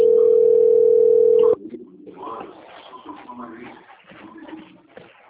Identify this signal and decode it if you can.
Ringback tone (North America), 440 + 480 Hz, with speech.
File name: out- - - - - .26715